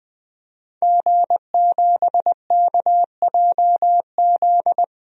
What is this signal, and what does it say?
Morse code, heard as a keyed tone, about 21 wpm, 700 Hz. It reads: G7KJZ
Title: DXCC-100 Call Signs - 5 Characters - 20wpm